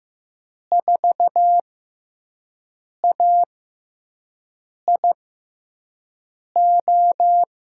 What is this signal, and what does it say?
Morse code, heard as a keyed tone, about 15 wpm, spaced as for 5 wpm, 700 Hz. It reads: 4AIO
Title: USA Call Signs - 6 Characters - 6x Character Spacing - 15wpm